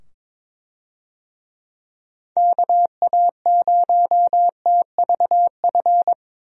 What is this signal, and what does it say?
Morse code, heard as a keyed tone, about 22 wpm, 700 Hz. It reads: KA0TVF
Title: USA Call Signs - 6 Characters - 22wpm